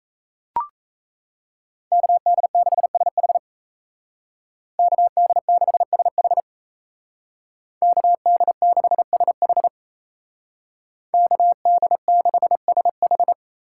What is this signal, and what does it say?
Morse code, heard as a keyed tone, about 37 wpm, 700 Hz. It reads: KD6SH KD6SH KD6SH KD6SH